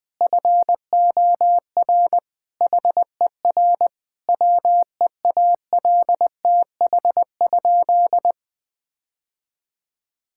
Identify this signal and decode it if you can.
Morse code, heard as a keyed tone, about 20 wpm, 700 Hz. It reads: FOR HER WEALTH?
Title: Sentences from Top 1600 Words 20wpm